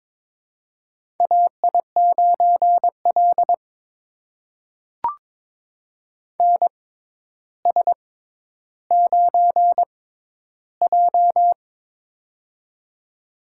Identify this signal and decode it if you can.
Morse code, heard as a keyed tone, about 22 wpm, spaced as for 11 wpm, 700 Hz. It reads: AI9L NS9J
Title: USA Call Signs - 4 Characters - 6x Character Spacing - 22wpm